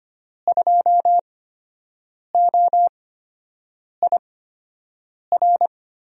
Morse code, heard as a keyed tone, about 25 wpm, spaced as for 7 wpm, 700 Hz. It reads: 2OIR
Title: USA Call Signs - 8x Character Spacing - 25wpm